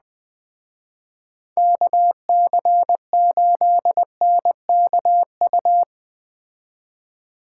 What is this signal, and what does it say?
Morse code, heard as a keyed tone, about 21 wpm, 700 Hz. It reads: KC8NKU